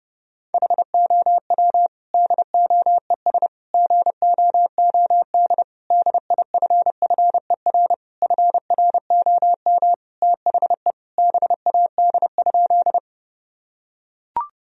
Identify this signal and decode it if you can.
Morse code, heard as a keyed tone, about 32 wpm, 700 Hz. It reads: HOW DOES GOOD DIFFER FROM THE BAD?
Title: Sentences from Top 1000 Words - Review Part 1 - 30wpm